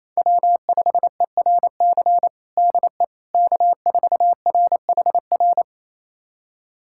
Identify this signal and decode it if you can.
Morse code, heard as a keyed tone, about 28 wpm, 700 Hz. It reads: W5ERC DE K4RHR
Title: USA Callsign DE Callsign 28wpm